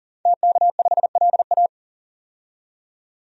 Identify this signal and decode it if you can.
Morse code, heard as a keyed tone, about 37 wpm, 700 Hz. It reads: TK5LA
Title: DXCC-100 Call Signs - 5 Characters - 40wpm